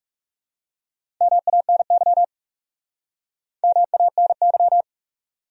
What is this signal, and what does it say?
Morse code, heard as a keyed tone, about 40 wpm, 700 Hz. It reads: MANY MANY